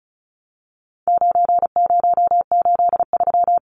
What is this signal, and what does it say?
Morse code, heard as a keyed tone, about 34 wpm, 700 Hz. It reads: 9083
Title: Numbers - 4-Digit 35wpm